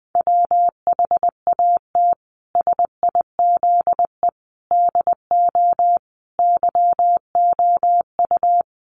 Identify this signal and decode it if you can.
Morse code, heard as a keyed tone, about 20 wpm, 700 Hz. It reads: WHAT SIZE DO YOU
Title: Sentences from Top 400 Words 20wpm